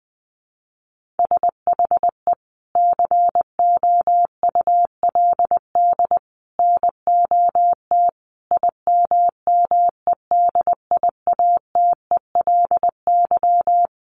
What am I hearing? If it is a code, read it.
Morse code, heard as a keyed tone, about 20 wpm, 700 Hz. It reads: SHE COULD NOT IMMEDIATELY
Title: Sentences from Top 1100 Words 20wpm